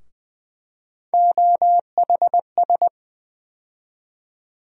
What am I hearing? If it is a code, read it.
Morse code, heard as a keyed tone, about 20 wpm, 700 Hz. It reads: OHS
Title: USA Call Sign Postfixes - 3 Letters 20wpm